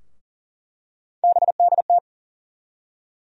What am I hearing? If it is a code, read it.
Morse code, heard as a keyed tone, about 38 wpm, 700 Hz. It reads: BDT
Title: USA Call Sign Postfixes - 3 Letters 40wpm